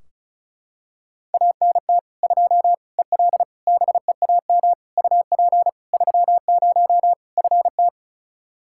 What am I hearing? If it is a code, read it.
Morse code, heard as a keyed tone, about 36 wpm, 700 Hz. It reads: ANT 2 EL BEAM UP 30 FT
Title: K1USN SST QSO Practice 35wpm